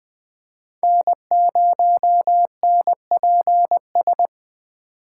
Morse code, heard as a keyed tone, about 20 wpm, 700 Hz. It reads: N0NPS